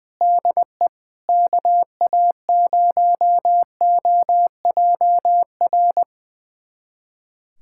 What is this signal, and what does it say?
Morse code, heard as a keyed tone, about 20 wpm, 690 Hz. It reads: DE KA0OJR
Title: USA Callsign DE Callsign 20wpm